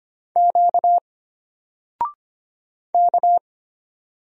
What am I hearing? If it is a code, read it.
Morse code, heard as a keyed tone, about 25 wpm, 700 Hz. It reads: Q K
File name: Single-Letter-Number-25wpm